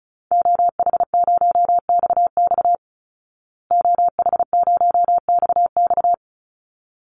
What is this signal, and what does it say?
Morse code, heard as a keyed tone, about 34 wpm, 700 Hz. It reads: OH0XX OH0XX
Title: Call Signs Repeated 3x - 35wpm